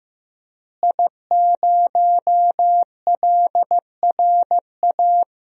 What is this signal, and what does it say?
Morse code, heard as a keyed tone, about 15 wpm, 700 Hz. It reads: I0LRA
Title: DXCC-100 Call Signs - 5 Characters - Repeated 3x - 15wpm